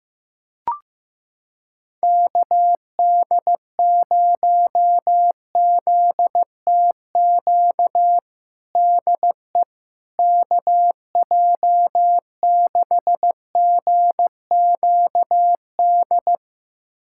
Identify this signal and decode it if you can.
Morse code, heard as a keyed tone, about 15 wpm, 700 Hz. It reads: KD0ZTQ DE KJ6GQD
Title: USA Callsign DE Callsign 15wpm